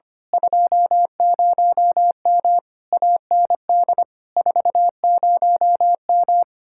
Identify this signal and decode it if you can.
Morse code, heard as a keyed tone, about 24 wpm, 700 Hz. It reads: 20M AND 40M
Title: Speed-Racing Common QSO Phrases - 25wpm